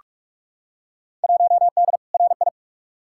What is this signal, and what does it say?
Morse code, heard as a keyed tone, about 46 wpm, 700 Hz. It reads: 1D RI
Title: Field Day Exchanges - 45wpm